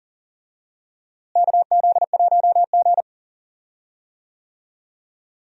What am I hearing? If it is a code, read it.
Morse code, heard as a keyed tone, about 40 wpm, 700 Hz. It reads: KZ1G
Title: USA Call Signs - 4 Characters - Repeated 3x - 40wpm